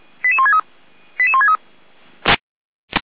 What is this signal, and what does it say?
Five-tone selective call (ZVEI), heard as quick runs of five tones, tones of about 70 ms. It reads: 88163 88163